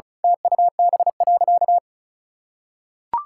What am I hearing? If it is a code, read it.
Morse code, heard as a keyed tone, about 36 wpm, 700 Hz. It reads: TUB.
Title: Pangram Challenge v2 - 35wpm